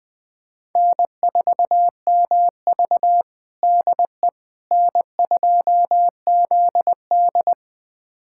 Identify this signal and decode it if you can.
Morse code, heard as a keyed tone, about 20 wpm, 700 Hz. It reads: N4MV DE N2ZD
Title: USA Callsign DE Callsign - 4 Characters - 20wpm